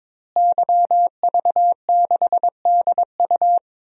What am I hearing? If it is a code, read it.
Morse code, heard as a keyed tone, about 22 wpm, 700 Hz. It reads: YV6DU
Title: DXCC-100 Call Signs - 5 Characters - Repeated 3x - 22wpm